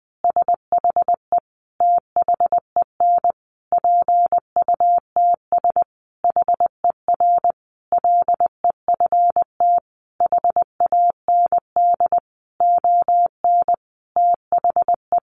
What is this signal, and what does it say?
Morse code, heard as a keyed tone, about 20 wpm, 700 Hz. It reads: SHE THEN PUTS HER LEFT HAND ON THE
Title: Easy Phrases 20wpm